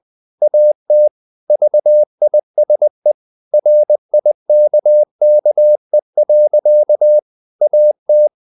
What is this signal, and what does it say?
Morse code, heard as a keyed tone, about 20 wpm, 600 Hz. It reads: AT VISE RIKKE. AT
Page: Ferie 20 wpm